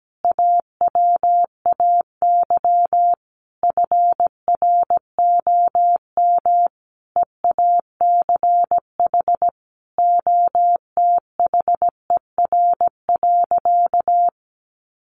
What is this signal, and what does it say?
Morse code, heard as a keyed tone, about 17 wpm, 700 Hz. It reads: AWAY FROM EACH OTHER.
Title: Easy Phrases 17wpm